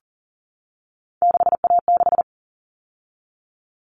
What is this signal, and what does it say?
Morse code, heard as a keyed tone, about 40 wpm, 700 Hz. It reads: 6A6
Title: All Cut Numbers - 3-Digit 40wpm